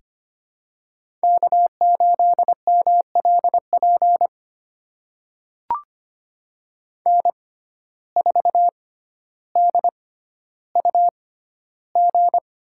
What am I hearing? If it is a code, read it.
Morse code, heard as a keyed tone, about 25 wpm, spaced as for 13 wpm, 700 Hz. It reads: K8MLP N4DUG
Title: USA Call Signs - 5 Characters - 6x Character Spacing - 25wpm